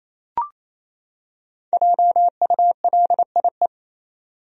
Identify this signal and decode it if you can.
Morse code, heard as a keyed tone, about 28 wpm, 700 Hz. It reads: JULIE
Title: USA Common Names - 28wpm